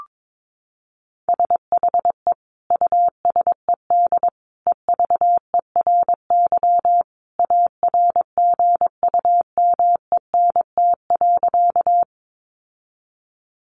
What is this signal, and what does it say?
Morse code, heard as a keyed tone, about 22 wpm, 700 Hz. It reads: SHE USED EVERY ARGUMENT.
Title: Sentences from Top 1400 Words 22wpm